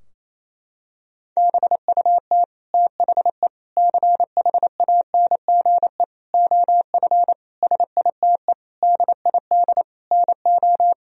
Morse code, heard as a keyed tone, about 28 wpm, 700 Hz. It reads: BUT THE CHANGE OF SITE DID NO
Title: Sentences from Top 1000 Words - Review Part 4 - 28wpm